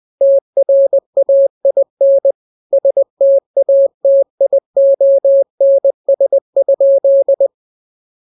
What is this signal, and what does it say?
Morse code, heard as a keyed tone, about 20 wpm, 550 Hz. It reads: TRAIN STATIONS?
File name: jokes / joke 20 20 0000